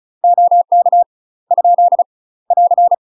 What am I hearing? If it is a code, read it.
Morse code, heard as a keyed tone, about 34 wpm, 700 Hz. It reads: OK ? <AR>
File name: UK MASTER QSOs TRAINING / 35WPM / UK MASTER QSOs 0005